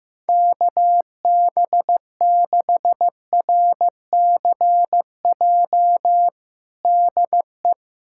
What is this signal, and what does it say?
Morse code, heard as a keyed tone, about 15 wpm, 700 Hz. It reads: KB6RCJ DE
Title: USA Callsign DE Callsign - 6 Characters - 15wpm